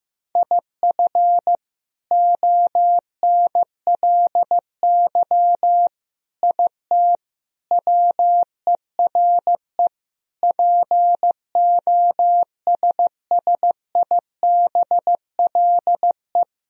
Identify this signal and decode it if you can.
Morse code, heard as a keyed tone, about 16 wpm, 700 Hz. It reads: IF ONLY IT WERE POSSIBLE
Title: Sentences from Top 500 Words and Abbreviations 15wpm